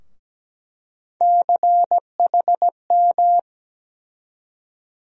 Morse code, heard as a keyed tone, about 17 wpm, 700 Hz. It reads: CHM
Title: USA Call Sign Postfixes - 3 Letters 17wpm